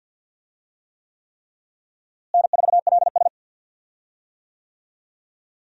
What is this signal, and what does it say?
Morse code, heard as a keyed tone, about 52 wpm, 700 Hz. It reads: N4LS
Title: USA Call Signs - 4 Characters - Repeated 3x - 50wpm